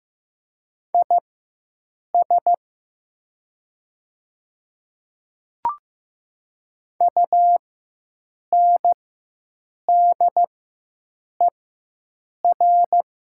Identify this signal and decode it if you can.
Morse code, heard as a keyed tone, about 15 wpm, spaced as for 7 wpm, 700 Hz. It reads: IS UNDER
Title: 4x Character Spacing - Top 200 Words - 15wpm